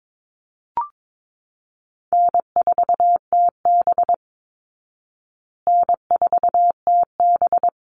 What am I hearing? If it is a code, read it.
Morse code, heard as a keyed tone, about 22 wpm, 690 Hz. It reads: N4TB N4TB
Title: Easy Call Signs Repeated 2x - 22wpm